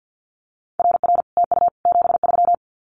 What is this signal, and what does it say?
Morse code, heard as a keyed tone, about 50 wpm, 700 Hz. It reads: RR TU 73
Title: Common QSO Phrases Encore 50wpm